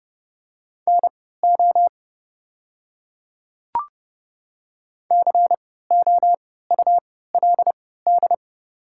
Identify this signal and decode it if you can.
Morse code, heard as a keyed tone, about 30 wpm, spaced as for 17 wpm, 700 Hz. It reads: NO COULD
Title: 3x Character Spacing - Top 100 Words - 30wpm